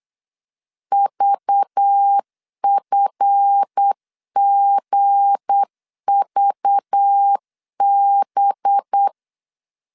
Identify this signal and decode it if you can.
Morse code, heard as a keyed tone, about 8 wpm, 800 Hz. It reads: VFGVB
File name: LEZIONI / 8 YJ45 / 8 YJ45 8wpm 3214